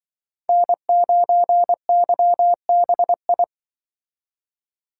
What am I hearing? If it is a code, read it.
Morse code, heard as a keyed tone, about 25 wpm, 700 Hz. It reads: N9YBI